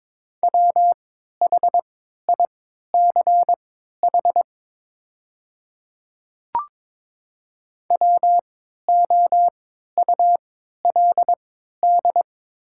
Morse code, heard as a keyed tone, about 22 wpm, spaced as for 13 wpm, 700 Hz. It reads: WHICH WOULD
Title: 3x Character Spacing - Top 100 Words - 22wpm